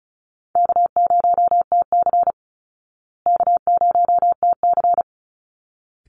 Morse code, heard as a keyed tone, about 34 wpm, 700 Hz. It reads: K0TC K0TC
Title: Easy Call Signs Repeated 2x - 35wpm